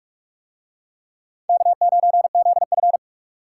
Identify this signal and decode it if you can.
Morse code, heard as a keyed tone, about 39 wpm, 700 Hz. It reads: K9ZF